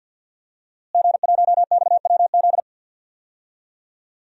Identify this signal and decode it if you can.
Morse code, heard as a keyed tone, about 46 wpm, 700 Hz. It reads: G1XWZ
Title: DXCC-100 Call Signs - 5 Characters - Repeated 3x - 50wpm